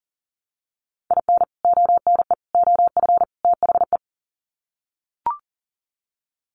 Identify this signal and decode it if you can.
Morse code, heard as a keyed tone, about 40 wpm, 700 Hz. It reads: IN ONE OF THE
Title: Sets of 4 Words 40wpm